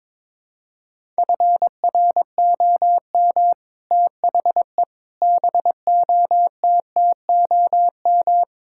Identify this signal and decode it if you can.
Morse code, heard as a keyed tone, about 22 wpm, 700 Hz. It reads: FROM THE BOTTOM
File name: practice / 22wpm / Sets of 6 Words - 22wpm